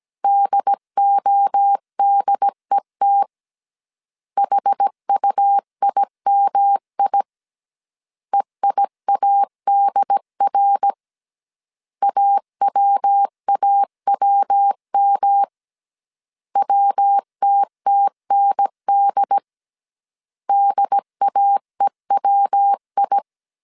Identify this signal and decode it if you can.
Morse code, heard as a keyed tone, about 17 wpm, 800 Hz. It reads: DODET HUIMI EIADR AWAWM WTTND DAEWI
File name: CW / DET / 1 3 1500C S C 16wpm 2148